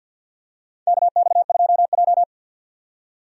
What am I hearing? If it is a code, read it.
Morse code, heard as a keyed tone, about 49 wpm, 700 Hz. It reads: KX2J
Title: USA Call Signs - 4 Characters - 50wpm